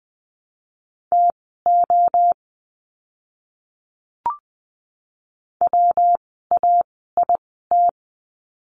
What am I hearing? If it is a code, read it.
Morse code, heard as a keyed tone, about 20 wpm, spaced as for 14 wpm, 700 Hz. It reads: TO WAIT
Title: 115 Lesson 19f - Letter w - Words 20wpm